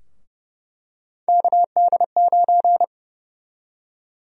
Morse code, heard as a keyed tone, about 31 wpm, 700 Hz. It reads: KD9
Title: USA Call Sign Prefixes - Repeated 2x - 30wpm